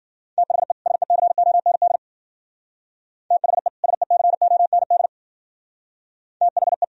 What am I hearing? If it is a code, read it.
Morse code, heard as a keyed tone, about 52 wpm, 700 Hz. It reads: THE SECOND THE SECOND THE